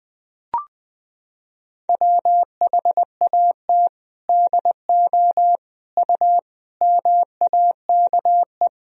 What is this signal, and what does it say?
Morse code, heard as a keyed tone, about 20 wpm, 700 Hz. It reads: WHAT DO U MAKE
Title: Sentences from Top 100 Words and Abbreviations 20wpm